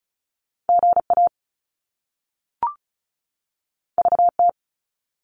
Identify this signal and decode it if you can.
Morse code, heard as a keyed tone, about 36 wpm, 700 Hz. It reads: GA VT
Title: US States 35wpm